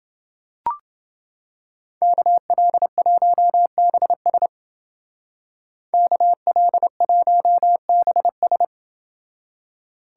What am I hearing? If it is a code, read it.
Morse code, heard as a keyed tone, about 29 wpm, 700 Hz. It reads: KL1BS KL1BS